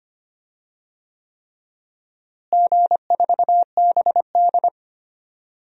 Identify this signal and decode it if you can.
Morse code, heard as a keyed tone, about 25 wpm, 700 Hz. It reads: G4BD
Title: DXCC-100 Call Signs - 4 Characters - Repeated 3x - 25wpm